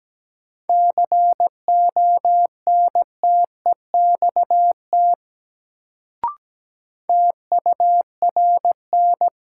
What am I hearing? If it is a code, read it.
Morse code, heard as a keyed tone, about 17 wpm, 700 Hz. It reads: CONTEXT TURN
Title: Sets of 4 Words - New Word Drill - 17wpm